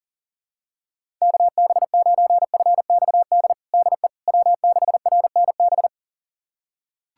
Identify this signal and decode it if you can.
Morse code, heard as a keyed tone, about 40 wpm, 700 Hz. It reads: KB9FXD DE W6RNB
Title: USA Callsign DE Callsign 40wpm